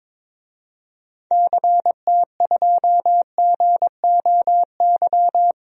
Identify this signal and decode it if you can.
Morse code, heard as a keyed tone, about 22 wpm, 700 Hz. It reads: CT2GOY
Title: DXCC-100 Call Signs - 6 Characters - Repeated 2x - 22wpm